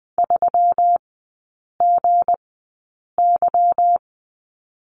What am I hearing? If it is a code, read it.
Morse code, heard as a keyed tone, about 20 wpm, spaced as for 8 wpm, 700 Hz. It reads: 3GY
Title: 159 Lesson 26e - Letter g - Three Characters 20wpm